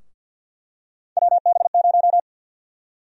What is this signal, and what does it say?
Morse code, heard as a keyed tone, about 52 wpm, 700 Hz. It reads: WB0
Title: USA Call Sign Prefixes - 50wpm